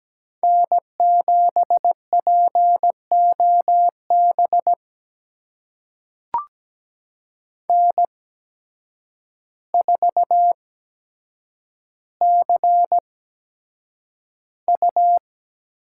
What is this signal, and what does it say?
Morse code, heard as a keyed tone, about 17 wpm, spaced as for 8 wpm, 700 Hz. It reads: N7POB N4CU
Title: USA Call Signs - 8x Character Spacing - 17wpm